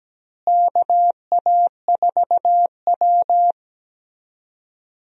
Morse code, heard as a keyed tone, about 17 wpm, 700 Hz. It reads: KA4W
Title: USA Call Signs - 4 Characters - Repeated 3x - 17wpm